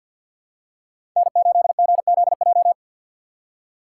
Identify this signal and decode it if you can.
Morse code, heard as a keyed tone, about 46 wpm, 700 Hz. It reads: N8GZJ
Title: USA Call Signs - 5 Characters - 3x Character Spacing - 50wpm